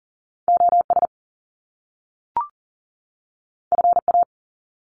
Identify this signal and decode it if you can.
Morse code, heard as a keyed tone, about 40 wpm, 700 Hz. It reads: OS FA
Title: Top Two Letter Combinations in Words 40wpm